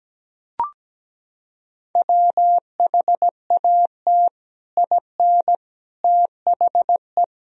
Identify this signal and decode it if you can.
Morse code, heard as a keyed tone, about 17 wpm, 700 Hz. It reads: WHAT IN THE
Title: Sentences from Top 1000 Words - Review Part 1 - 17wpm